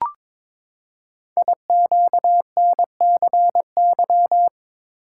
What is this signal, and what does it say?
Morse code, heard as a keyed tone, about 22 wpm, 700 Hz. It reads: IQNCY
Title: Code Groups - 5 Letters - 22wpm